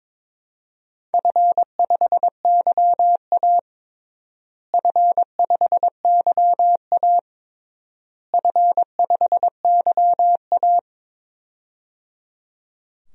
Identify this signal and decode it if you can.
Morse code, heard as a keyed tone, about 22 wpm, 700 Hz. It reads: F5YA F5YA F5YA
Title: DXCC-100 Call Signs - 4 Characters - Repeated 3x - 22wpm